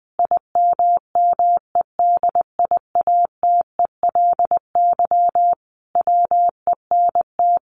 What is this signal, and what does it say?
Morse code, heard as a keyed tone, about 19 wpm, 700 Hz. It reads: IMMEDIATELY WENT
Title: Sentences from Top 1100 Words 20wpm